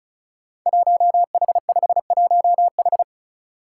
Morse code, heard as a keyed tone, about 36 wpm, 700 Hz. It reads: 1H51H
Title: Five Challenging Letter-Number Combinations 35wpm